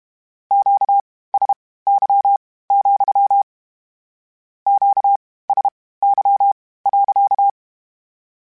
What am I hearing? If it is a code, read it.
Morse code, heard as a keyed tone, about 32 wpm, spaced as for 18 wpm, 800 Hz. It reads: QSY, QSY.
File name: Q Calls / Q Calls-mp3s / Q Calls sppc @30